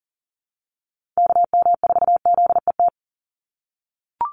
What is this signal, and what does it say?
Morse code, heard as a keyed tone, about 40 wpm, 700 Hz. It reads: KM4ZET
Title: USA Call Signs 40wpm